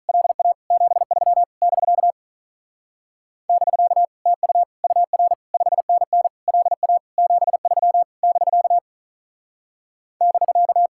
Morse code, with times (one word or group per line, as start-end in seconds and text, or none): 0.1-0.5 LA
0.7-1.4 73
1.6-2.1 <BK>
3.5-4.0 <BK>
4.3-4.6 TU
4.8-5.3 UR
5.5-6.3 5NN
6.5-7.0 LA
7.2-8.0 73
8.2-8.8 <BK>
10.2-10.9 <BK>